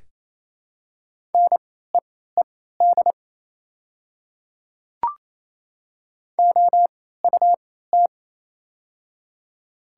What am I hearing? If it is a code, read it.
Morse code, heard as a keyed tone, about 27 wpm, spaced as for 16 wpm, 700 Hz. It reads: NEED OUT
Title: 3x Character Spacing - Top 300 Words - 28wpm